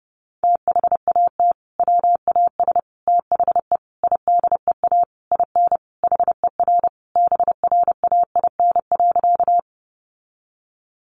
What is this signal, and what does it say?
Morse code, heard as a keyed tone, about 30 wpm, 700 Hz. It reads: THAT WAS THE IDEA IN HER BRAIN.
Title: Sentences from Top 1100 Words 30wpm